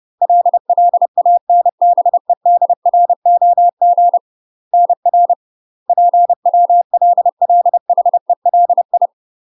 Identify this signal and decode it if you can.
Morse code, heard as a keyed tone, about 30 wpm, 700 Hz. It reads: LLANBEDROG NR PWLLHELI
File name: UK MASTER QSOs TRAINING / 30WPM / UK MASTER QSOs 0001